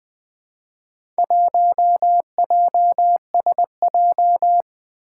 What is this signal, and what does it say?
Morse code, heard as a keyed tone, about 20 wpm, 700 Hz. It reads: 1JSJ
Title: Four Challenging Letter-Number Combinations 20wpm